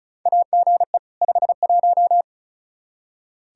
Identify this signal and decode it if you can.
Morse code, heard as a keyed tone, about 36 wpm, 700 Hz. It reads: AGE 51
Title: Practice QSO - Exchange 3 - More Info - 35wpm